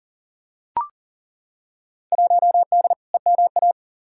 Morse code, heard as a keyed tone, about 40 wpm, 700 Hz. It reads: 1D EMA
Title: Field Day Exchanges - 40wpm